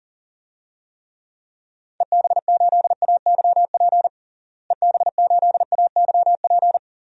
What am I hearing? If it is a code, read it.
Morse code, heard as a keyed tone, about 40 wpm, 700 Hz. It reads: EB8AYP EB8AYP
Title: DXCC-100 Call Signs - 6 Characters - Repeated 3x - 40wpm